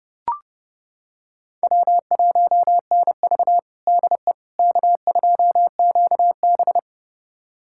Morse code, heard as a keyed tone, about 30 wpm, 700 Hz. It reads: W1NV DE K2QB
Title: USA Callsign DE Callsign - 4 Characters - 30wpm